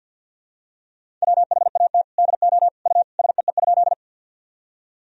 Morse code, heard as a keyed tone, about 49 wpm, 700 Hz. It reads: WHAT DO U SEE?
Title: Sentences from Top 100 Words and Abbreviations 50wpm